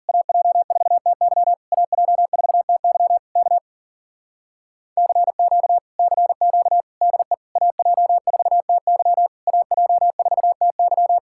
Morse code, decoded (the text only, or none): AJ4TY AJ4TY K CQ CQ DE AJ4TY AJ4TY